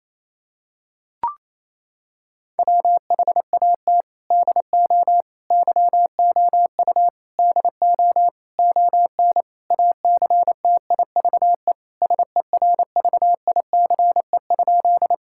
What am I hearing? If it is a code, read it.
Morse code, heard as a keyed tone, about 28 wpm, 700 Hz. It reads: WHAT DO YOU DO ON ACTIVE SERVICE?
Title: Sentences from Top 1000 Words - Review Part 6 - 28wpm